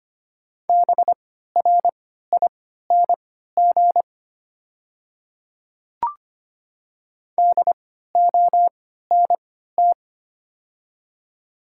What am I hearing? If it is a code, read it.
Morse code, heard as a keyed tone, about 25 wpm, spaced as for 14 wpm, 700 Hz. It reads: BRING DONT
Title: 3x Character Spacing - Top 500 Words - 25wpm